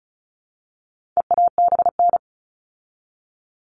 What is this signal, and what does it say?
Morse code, heard as a keyed tone, about 34 wpm, 700 Hz. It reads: EABN
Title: All Cut Numbers - 4-Digit 35wpm